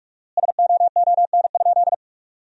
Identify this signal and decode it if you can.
Morse code, heard as a keyed tone, about 46 wpm, 700 Hz. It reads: SOON?
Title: 4x Word-Spacing - Sentences from Top 400 Words - 45wpm